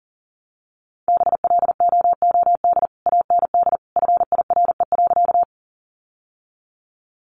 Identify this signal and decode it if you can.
Morse code, heard as a keyed tone, about 40 wpm, 700 Hz. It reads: BLOOD AND FIRE.
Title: Sentences from Top 1100 Words 40wpm